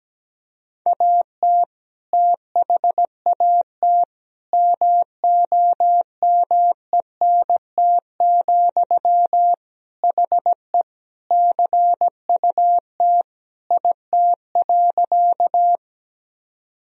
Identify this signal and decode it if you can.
Morse code, heard as a keyed tone, about 17 wpm, 700 Hz. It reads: AT THAT MOMENT, HE CUT IT.
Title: Sentences from Top 1000 Words - Review Part 1 - 17wpm